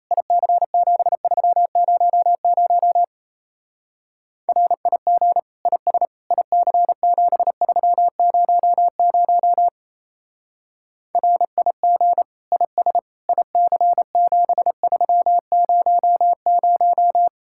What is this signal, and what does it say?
Morse code, heard as a keyed tone, about 35 wpm, 700 Hz. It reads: IC7300 RIG IS IC7300 RIG IS IC7300
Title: Speed-Racing Common QSO Phrases - 28wpm